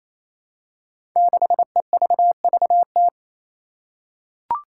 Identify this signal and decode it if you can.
Morse code, heard as a keyed tone, about 28 wpm, 700 Hz. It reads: 6EVVT
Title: Five Challenging Letter-Number Combinations 28wpm